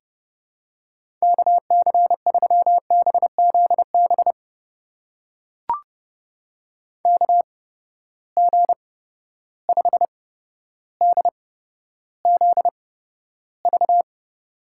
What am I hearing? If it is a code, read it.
Morse code, heard as a keyed tone, about 30 wpm, spaced as for 13 wpm, 700 Hz. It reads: KC3BZB KG5DZV
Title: USA Call Signs - 6 Characters - 8x Character Spacing - 30wpm